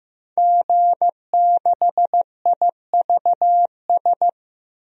Morse code, heard as a keyed tone, about 15 wpm, 700 Hz. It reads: G6IVS
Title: DXCC-100 Call Signs - 5 Characters - Repeated 3x - 15wpm